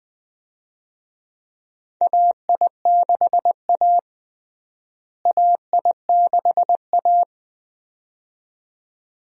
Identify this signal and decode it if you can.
Morse code, heard as a keyed tone, about 20 wpm, 700 Hz. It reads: AI6A AI6A
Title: USA Call Signs - 4 Characters - Repeated 2x - 20wpm